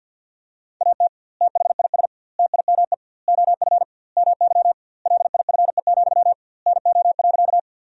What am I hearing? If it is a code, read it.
Morse code, heard as a keyed tone, about 53 wpm, 700 Hz. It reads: AT THIS TIME OF MY LIFE, NO.
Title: 2x Word-Spacing - Sentences from Top 300 Words - 50wpm